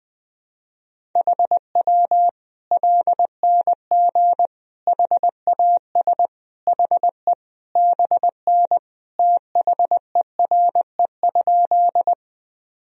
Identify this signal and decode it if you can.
Morse code, heard as a keyed tone, about 20 wpm, 700 Hz. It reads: HW LNG HAS HE BN THERE?
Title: Sentences from Top 100 Words and Abbreviations 20wpm